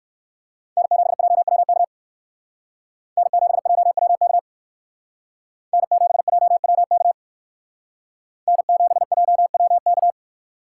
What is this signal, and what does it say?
Morse code, heard as a keyed tone, about 64 wpm, 700 Hz. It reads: N7JWK N7JWK N7JWK N7JWK